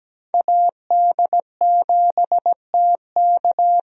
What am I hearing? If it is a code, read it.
Morse code, heard as a keyed tone, about 17 wpm, 700 Hz. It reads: AD7TK
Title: USA Call Signs - 5 Characters - 17wpm